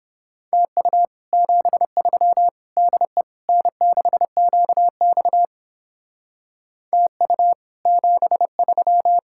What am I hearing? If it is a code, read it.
Morse code, heard as a keyed tone, about 29 wpm, 700 Hz. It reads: TU 73 DE N6QX TU 73